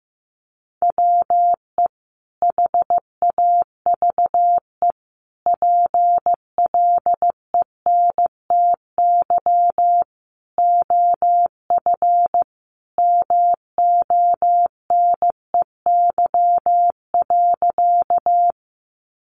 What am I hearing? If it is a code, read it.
Morse code, heard as a keyed tone, about 15 wpm, 700 Hz. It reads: WE HAVE PLENTY OF MONEY.
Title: Sentences from Top 1300 Words 15wpm